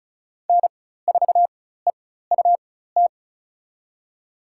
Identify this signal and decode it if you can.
Morse code, heard as a keyed tone, about 36 wpm, spaced as for 16 wpm, 700 Hz. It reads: N4EUT
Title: USA Call Signs - 4x Character Spacing - 35wpm